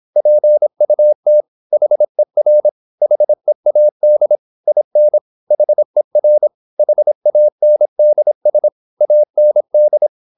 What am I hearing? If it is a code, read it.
Morse code, heard as a keyed tone, about 26 wpm, 600 Hz. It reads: PUT HER HEAD IN HER HANDS AND
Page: Never 26wpm | CW med Gnister